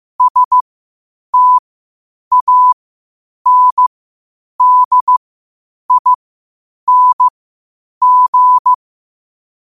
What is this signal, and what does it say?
Morse code, heard as a keyed tone, about 15 wpm, spaced as for 8 wpm, 1000 Hz. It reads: STANDING
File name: qotdfiles / 8wpm